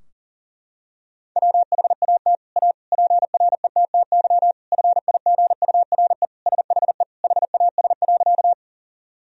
Morse code, heard as a keyed tone, about 40 wpm, 700 Hz. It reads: WHAT A PRETTY FIGURE SHE HAS.
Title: Sentences from Top 1000 Words - Review Part 2 - 40wpm